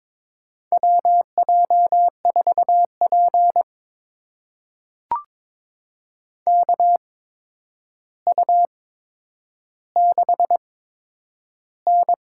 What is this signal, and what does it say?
Morse code, heard as a keyed tone, about 22 wpm, spaced as for 9 wpm, 700 Hz. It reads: WJ4P KU6N
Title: USA Call Signs - 4 Characters - 8x Character Spacing - 22wpm